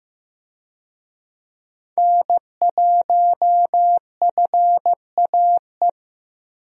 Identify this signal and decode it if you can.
Morse code, heard as a keyed tone, about 15 wpm, 700 Hz. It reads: N1FAE
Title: USA Call Signs - 5 Characters - Repeated 3x - 15wpm